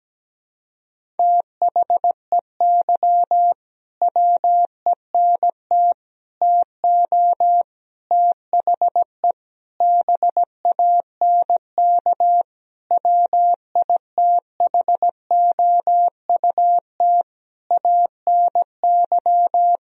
Morse code, heard as a keyed tone, about 17 wpm, 700 Hz. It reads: THEY WENT TO THE BANK WITHOUT ANY
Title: Sentences from Top 1000 Words - Review Part 4 - 17wpm